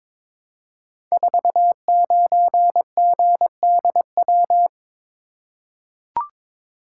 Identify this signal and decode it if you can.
Morse code, heard as a keyed tone, about 22 wpm, 700 Hz. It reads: 49GDW
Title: Code Groups - 5 Letter:Numbers - 22wpm